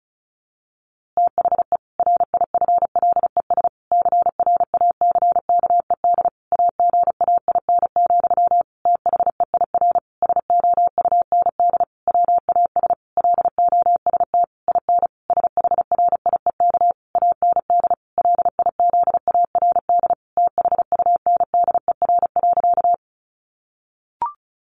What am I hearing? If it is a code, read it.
Morse code, heard as a keyed tone, about 34 wpm, 700 Hz. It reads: THE RIFLES CRACKED AGAIN, THEIR SOUND WAS LOST IN SHRIEK AND LIZARD THUNDER.
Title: A Sound Of Thunder 35wpm